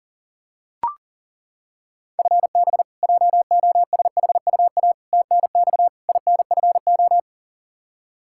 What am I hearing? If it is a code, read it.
Morse code, heard as a keyed tone, about 40 wpm, 700 Hz. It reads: FB JOSHUA TNX INFO
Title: K1USN SST QSO Practice 40wpm